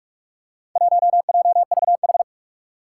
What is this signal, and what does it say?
Morse code, heard as a keyed tone, about 47 wpm, 700 Hz. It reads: 1JVH
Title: Four Challenging Letter-Number Combinations 45wpm